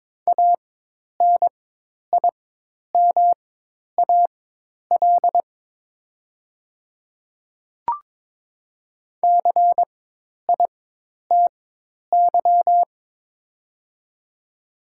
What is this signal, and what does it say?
Morse code, heard as a keyed tone, about 22 wpm, spaced as for 10 wpm, 700 Hz. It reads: ANIMAL CITY
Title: 4x Character Spacing - Top 300 Words - 22wpm